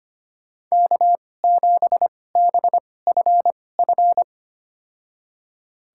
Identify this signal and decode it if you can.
Morse code, heard as a keyed tone, about 25 wpm, spaced as for 18 wpm, 700 Hz. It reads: K7BFF
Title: USA Call Signs - 5 Characters - 2x Character Spacing - 25wpm